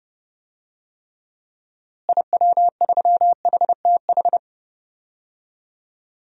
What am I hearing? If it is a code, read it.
Morse code, heard as a keyed tone, about 30 wpm, 700 Hz. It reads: IW3HTH
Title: DXCC-100 Call Signs - 6 Characters - Repeated 3x - 30wpm